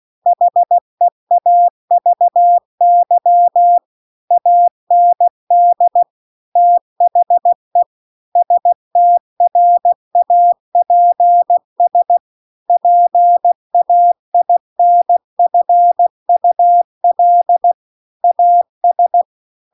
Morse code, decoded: HEAVY AND THE STRAPS PAINFUL AS

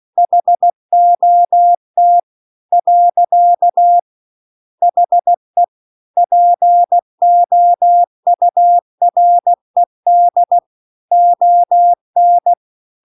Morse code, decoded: HOT . HE POURED ON

